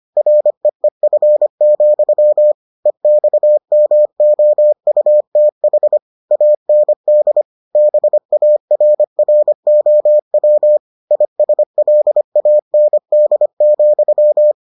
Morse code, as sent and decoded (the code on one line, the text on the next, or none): .-. . ..-. --..-- / . -..- -- --- ..- - .... / .- -. -.. / -... .- .-. .-. --- .-- / .. ... .-.. .- -. -.. --..--
REEF, EXMOUTH AND BARROW ISLAND,